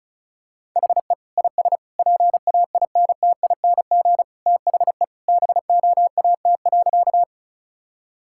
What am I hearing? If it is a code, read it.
Morse code, heard as a keyed tone, about 34 wpm, 700 Hz. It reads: HE IS PAINTING THE BOAT.
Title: Sentences from Top 1000 Words - Review Part 6 - 35wpm